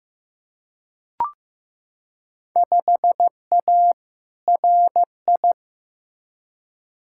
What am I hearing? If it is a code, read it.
Morse code, heard as a keyed tone, about 15 wpm, 700 Hz. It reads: 5A RI
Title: Field Day Exchanges Encore - 15wpm